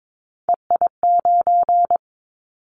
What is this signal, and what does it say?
Morse code, heard as a keyed tone, about 22 wpm, 700 Hz. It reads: EI9
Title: Callsign Prefixes for Top 50 DXCC - 22wpm